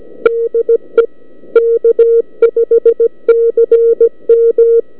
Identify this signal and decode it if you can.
Morse code, heard as a keyed tone, about 17 wpm, 460 Hz. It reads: DE K5CM